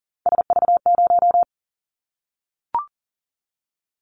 Morse code, heard as a keyed tone, about 39 wpm, 700 Hz. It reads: SV0
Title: Callsign Prefixes for Top 50 DXCC - 40wpm